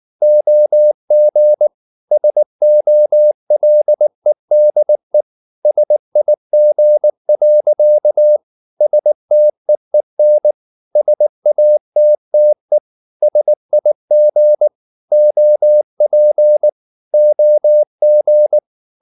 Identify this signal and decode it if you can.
Morse code, heard as a keyed tone, about 19 wpm, 600 Hz. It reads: OG SOLEDE SIG. STEEN SATTE SIG OP OG
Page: Ferie 19 wpm